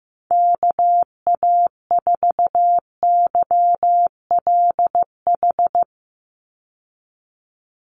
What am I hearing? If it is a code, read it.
Morse code, heard as a keyed tone, about 15 wpm, 700 Hz. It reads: KA4YLH
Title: USA Call Signs 15wpm